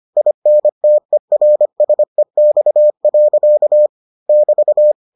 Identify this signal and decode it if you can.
Morse code, heard as a keyed tone, about 25 wpm, 600 Hz. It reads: INTERSEX. <BT>